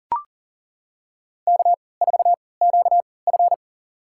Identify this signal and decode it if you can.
Morse code, heard as a keyed tone, about 40 wpm, spaced as for 23 wpm, 700 Hz. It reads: K4QF
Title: USA Call Signs - 4 Characters - 3x Character Spacing - 40wpm